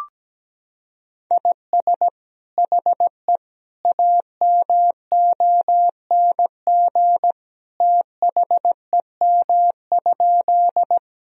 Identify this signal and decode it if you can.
Morse code, heard as a keyed tone, about 17 wpm, 700 Hz. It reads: IS HE AMONG THEM?
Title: Sentences from Top 1000 Words - Review Part 3 - 17wpm